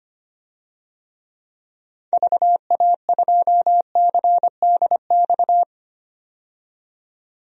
Morse code, heard as a keyed tone, about 25 wpm, 700 Hz. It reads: VA2CDX
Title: DXCC-100 Call Signs - 6 Characters - 25wpm